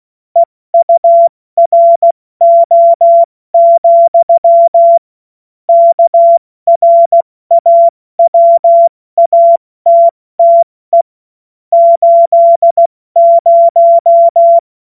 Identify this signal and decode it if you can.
Morse code, heard as a keyed tone, about 16 wpm, 680 Hz. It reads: EURO, KRAWATTE 80
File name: Witz-30-September-2025-70-F80